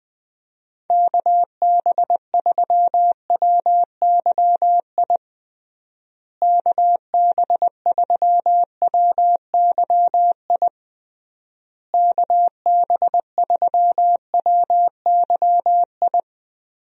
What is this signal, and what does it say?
Morse code, heard as a keyed tone, about 20 wpm, 700 Hz. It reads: KB3WYI KB3WYI KB3WYI
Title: USA Call Signs - 6 Characters - Repeated 3x - 20wpm